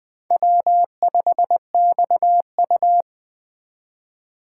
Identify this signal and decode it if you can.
Morse code, heard as a keyed tone, about 20 wpm, 700 Hz. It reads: W5XU
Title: Speed Racing - USA 1x2 Call Signs 17wpm